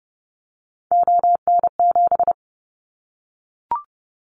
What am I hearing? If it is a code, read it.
Morse code, heard as a keyed tone, about 30 wpm, 700 Hz. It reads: ON7
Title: Callsign Prefixes for Top 25 DXCC - Repeated 3x - 30wpm